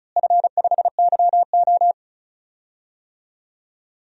Morse code, heard as a keyed tone, about 34 wpm, 700 Hz. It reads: F5YO
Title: DXCC-100 Call Signs - 4 Characters - Repeated 2x - 35wpm